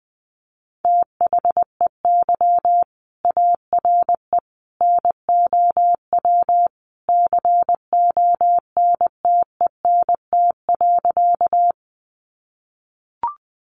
Morse code, heard as a keyed tone, about 20 wpm, 700 Hz. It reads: THEY ARE NOW CONTENT.
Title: Sentences from Top 900 Words 20wpm